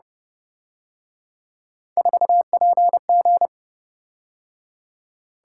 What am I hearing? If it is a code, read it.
Morse code, heard as a keyed tone, about 29 wpm, 700 Hz. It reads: 4PG